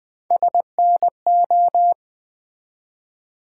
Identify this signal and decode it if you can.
Morse code, heard as a keyed tone, about 20 wpm, 700 Hz. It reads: SNO
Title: Build A Word - 1 Letter At A Time - 3 to 5-Letter Words - 20wpm